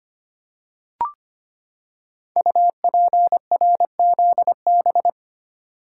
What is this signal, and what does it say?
Morse code, heard as a keyed tone, about 25 wpm, 700 Hz. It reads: UPRZB
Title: Code Groups - 5 Letters - 25wpm